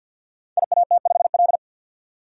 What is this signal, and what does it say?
Morse code, heard as a keyed tone, about 52 wpm, 700 Hz. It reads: IAT5L
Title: Code Groups - 5 Letter:Numbers - 50wpm